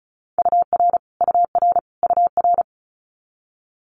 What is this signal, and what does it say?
Morse code, heard as a keyed tone, about 36 wpm, 700 Hz. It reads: UR UR UR
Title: Multi-letter Phonemes Repeated 3x 35wpm